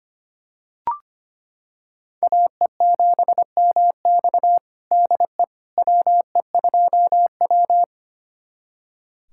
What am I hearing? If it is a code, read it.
Morse code, heard as a keyed tone, about 25 wpm, 700 Hz. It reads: AE7MX DE WE2W
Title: USA Callsign DE Callsign 25wpm